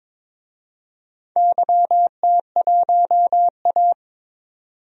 Morse code, heard as a keyed tone, about 22 wpm, 700 Hz. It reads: YT1A
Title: DXCC-100 Call Signs - 4 Characters - 22wpm